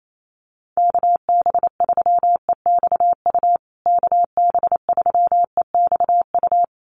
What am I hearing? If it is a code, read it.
Morse code, heard as a keyed tone, about 28 wpm, 700 Hz. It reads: KB3EXU KB3EXU
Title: USA Call Signs Repeated 2x 28wpm